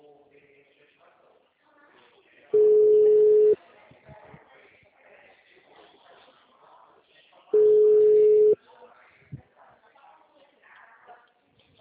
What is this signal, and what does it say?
Ringback tone (Europe, 425 Hz), with speech.